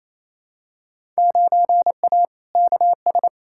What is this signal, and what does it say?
Morse code, heard as a keyed tone, about 28 wpm, 700 Hz. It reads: 9A KS
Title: Field Day Exchanges Encore - 28wpm